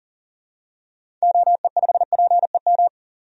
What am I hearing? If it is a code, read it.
Morse code, heard as a keyed tone, about 40 wpm, 700 Hz. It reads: OE5PEM
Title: DXCC-100 Call Signs - 6 Characters - 40wpm